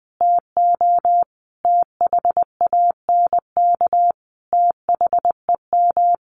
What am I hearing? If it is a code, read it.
Morse code, heard as a keyed tone, about 20 wpm, 700 Hz. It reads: TO THANK THEM
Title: Sentences from Top 1400 Words 20wpm